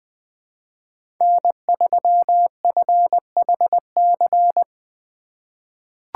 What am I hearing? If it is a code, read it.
Morse code, heard as a keyed tone, about 20 wpm, 700 Hz. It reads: N3FHC
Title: USA Call Signs - 8x Character Spacing - 20wpm